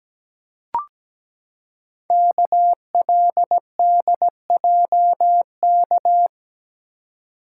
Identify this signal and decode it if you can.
Morse code, heard as a keyed tone, about 17 wpm, 700 Hz. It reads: KLDJK
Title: Code Groups - 5 Letters - 17wpm